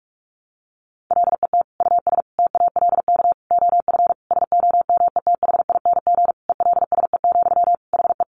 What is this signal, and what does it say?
Morse code, heard as a keyed tone, about 44 wpm, 700 Hz. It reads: LET US TALK OF SOMETHING ELSE, HE
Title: Sentences from Top 800 Words 45wpm